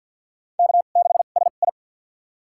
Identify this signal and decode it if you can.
Morse code, heard as a keyed tone, about 49 wpm, spaced as for 35 wpm, 700 Hz. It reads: K6SI
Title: USA Call Signs - 2x Character Spacing - 50wpm